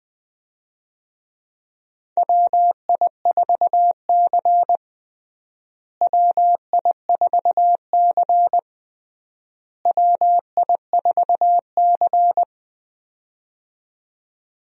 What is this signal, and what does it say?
Morse code, heard as a keyed tone, about 20 wpm, 700 Hz. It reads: WI4C WI4C WI4C
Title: USA Call Signs - 4 Characters - Repeated 3x - 20wpm